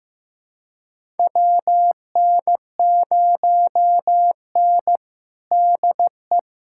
Morse code, heard as a keyed tone, about 15 wpm, 700 Hz. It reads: WN0N DE
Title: USA Callsign DE Callsign - 4 Characters - 15wpm